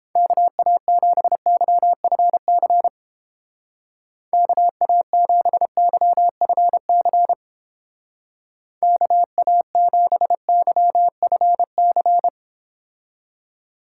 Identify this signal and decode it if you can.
Morse code, heard as a keyed tone, about 29 wpm, 700 Hz. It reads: KA7YFC KA7YFC KA7YFC